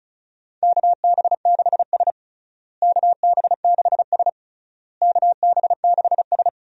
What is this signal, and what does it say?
Morse code, heard as a keyed tone, about 36 wpm, 700 Hz. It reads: KB6S KB6S KB6S
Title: USA Call Signs - 4 Characters - Repeated 3x - 35wpm